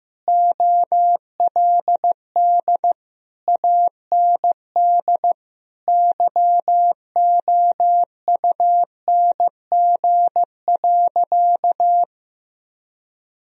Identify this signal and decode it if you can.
Morse code, heard as a keyed tone, about 15 wpm, 700 Hz. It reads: OLD AND YOUNG.
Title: Sentences from Top 1000 Words - Review Part 3 - 15wpm